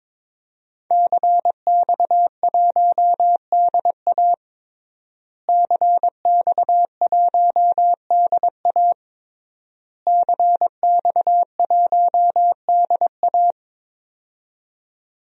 Morse code, heard as a keyed tone, about 22 wpm, 700 Hz. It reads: CX1DA CX1DA CX1DA
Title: DXCC-100 Call Signs - 5 Characters - Repeated 3x - 22wpm